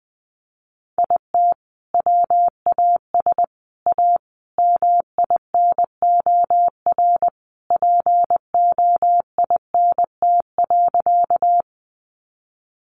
Morse code, heard as a keyed tone, about 20 wpm, 700 Hz. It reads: IT WAS A MINOR POINT.
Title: Sentences from Top 1500 Words 20wpm